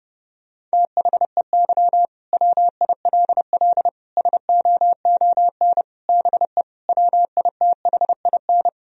Morse code, heard as a keyed tone, about 30 wpm, 700 Hz. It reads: THEY WILL SOON BE WITHIN
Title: Sentences from Top 1000 Words - Review Part 4 - 30wpm